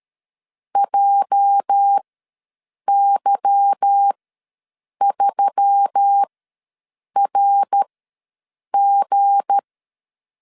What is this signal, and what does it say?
Morse code, heard as a keyed tone, about 13 wpm, spaced as for 7 wpm, 800 Hz. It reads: JY3RG
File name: LEZIONI / 8 YJ45 / 8 YJ ws 12wpm 3342